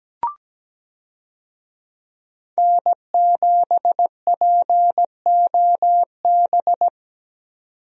Morse code, heard as a keyed tone, about 17 wpm, 700 Hz. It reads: N7POB
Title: USA Call Signs - 5 Characters - 17wpm